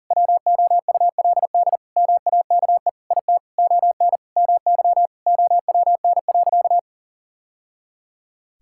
Morse code, heard as a keyed tone, about 40 wpm, 700 Hz. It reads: WOULD MAKE IT ON MY OWN.
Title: Sentences from Top 1000 Words - Review Part 6 - 40wpm